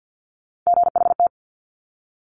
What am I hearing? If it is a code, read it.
Morse code, heard as a keyed tone, about 50 wpm, 700 Hz. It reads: GHT
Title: Top Three Letter Combinations in Words 50wpm